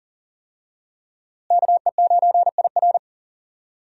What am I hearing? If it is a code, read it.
Morse code, heard as a keyed tone, about 40 wpm, 700 Hz. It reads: KE9IR
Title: USA Call Signs - 5 Characters - Repeated 2x - 40wpm